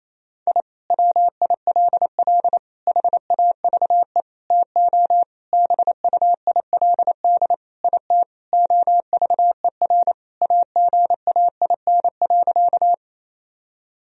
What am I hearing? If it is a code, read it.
Morse code, heard as a keyed tone, about 28 wpm, 700 Hz. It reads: I WILL HAVE TO BUILD IT OVER AGAIN.
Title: Sentences from Top 1000 Words - Review Part 1 - 28wpm